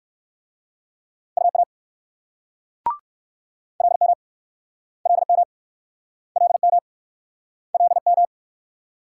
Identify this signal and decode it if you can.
Morse code, heard as a keyed tone, about 54 wpm, 700 Hz. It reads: UA LM LM LM LM
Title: Speed-Racing Multi-letter Phonemes - 45wpm